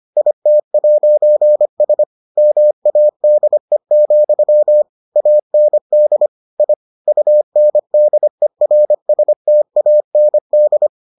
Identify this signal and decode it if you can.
Morse code, heard as a keyed tone, about 25 wpm, 600 Hz. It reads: IT'S MADE, AND I UNDERSTAND